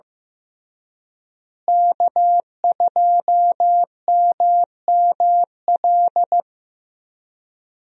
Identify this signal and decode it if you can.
Morse code, heard as a keyed tone, about 15 wpm, 700 Hz. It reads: K2MML
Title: USA Call Signs - 5 Characters - Repeated 3x - 15wpm